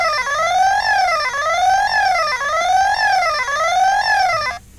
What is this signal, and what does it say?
There is no code.